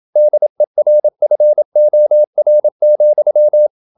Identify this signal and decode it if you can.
Morse code, heard as a keyed tone, about 27 wpm, 600 Hz. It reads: DERFOR,
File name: Ferie QRQ 27wpm